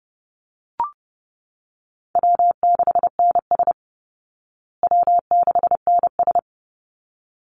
Morse code, heard as a keyed tone, about 30 wpm, 700 Hz. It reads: W6NS W6NS
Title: Call Signs Repeated 3x - 30wpm